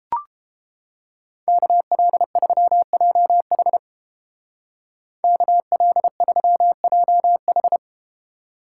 Morse code, heard as a keyed tone, about 31 wpm, 700 Hz. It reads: KL3JH KL3JH